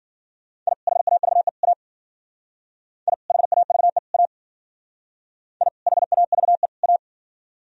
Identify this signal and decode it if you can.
Morse code, heard as a keyed tone, about 54 wpm, 700 Hz. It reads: I HAVE A I HAVE A I HAVE A